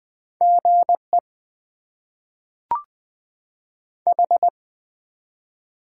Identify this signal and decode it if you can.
Morse code, heard as a keyed tone, about 20 wpm, 700 Hz. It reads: GE H
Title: Top 20 QSO Elements - 8x Character Spacing - 20wpm